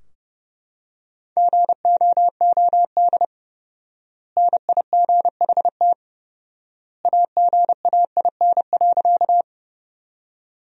Morse code, heard as a keyed tone, about 29 wpm, 700 Hz. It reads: GOOD NIGHT AGAIN.
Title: 4x Word-Spacing - Sentences from Top 300 Words - 30wpm